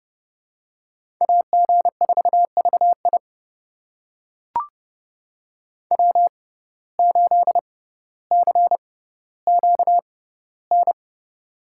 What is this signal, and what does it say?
Morse code, heard as a keyed tone, about 31 wpm, spaced as for 16 wpm, 700 Hz. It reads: AG4VI W8CQN
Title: USA Call Signs - 5 Characters - 6x Character Spacing - 30wpm